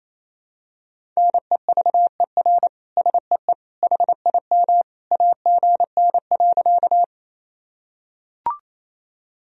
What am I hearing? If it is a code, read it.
Morse code, heard as a keyed tone, about 28 wpm, 700 Hz. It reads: NEVER SEE HIM AGN.
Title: Sentences from Top 300 Words and Abbreviations 28wpm